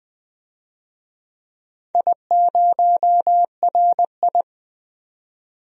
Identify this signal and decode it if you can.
Morse code, heard as a keyed tone, about 20 wpm, 700 Hz. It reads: I0RI
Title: DXCC-100 Call Signs - 4 Characters - 20wpm